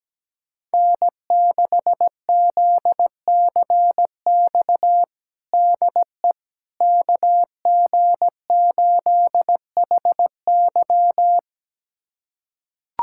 Morse code, heard as a keyed tone, about 17 wpm, 700 Hz. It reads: N6ZCX DE KG8HY
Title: USA Callsign DE Callsign 17wpm